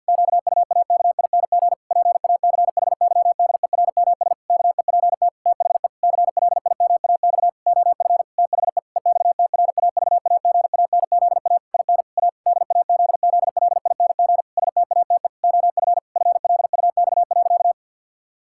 Morse code, heard as a keyed tone, about 52 wpm, 700 Hz. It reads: QUAKING PACHYDERMS KEPT THE CLIMAX OF THE EXTRAVAGANZA IN A DAZZLING STATE OF FLUX.
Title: Pangram Challenge v2 - 50wpm